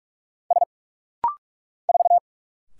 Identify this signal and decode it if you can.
Morse code, heard as a keyed tone, about 46 wpm, 700 Hz. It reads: S 4
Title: Single Letter-Number Alt - Rapid-Fire 45wpm